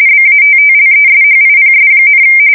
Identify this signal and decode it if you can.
RTTY, about 45.45 baud. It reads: BATTLESHIPS,
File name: fsk170 / battleshipnj